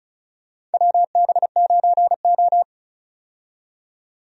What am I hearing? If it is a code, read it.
Morse code, heard as a keyed tone, about 34 wpm, 700 Hz. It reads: WB9O
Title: USA Call Signs - 4 Characters - 35wpm